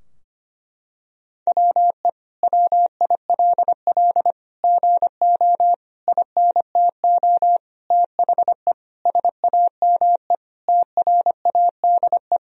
Morse code, heard as a keyed tone, about 25 wpm, 700 Hz. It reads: WE WILL GO INTO THE SAME TRADE
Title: Sentences from Top 1000 Words - Review Part 4 - 25wpm